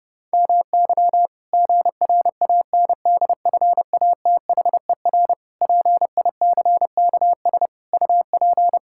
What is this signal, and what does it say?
Morse code, heard as a keyed tone, about 30 wpm, 700 Hz. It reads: MY GRANDFATHER PICKS UP
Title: Pangram Challenge v2 - 30wpm